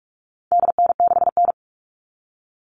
Morse code, heard as a keyed tone, about 44 wpm, 700 Hz. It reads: DN6N
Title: All Cut Numbers - 4-Digit 45wpm